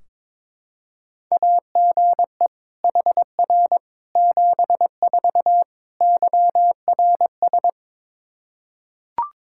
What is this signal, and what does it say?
Morse code, heard as a keyed tone, about 22 wpm, 700 Hz. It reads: AGE HR 74 YRS
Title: Practice QSO - Exchange 3 - More Info - 22wpm